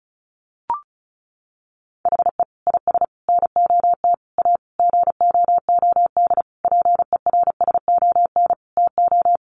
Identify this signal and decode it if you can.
Morse code, heard as a keyed tone, about 34 wpm, 700 Hz. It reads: HE IS NOT A GOOD PERSON TO
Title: Sentences from Top 1400 Words 35wpm